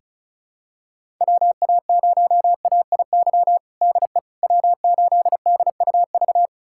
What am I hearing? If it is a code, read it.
Morse code, heard as a keyed tone, about 36 wpm, 700 Hz. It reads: WA0AIY DE W8DUV
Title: USA Callsign DE Callsign 35wpm